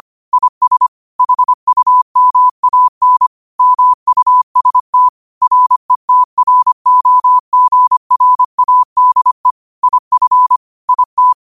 Morse code, heard as a keyed tone, about 25 wpm, 1000 Hz. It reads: IS HUMAN MUST RETROGRADE IF IT
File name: qotdfiles / 25wpm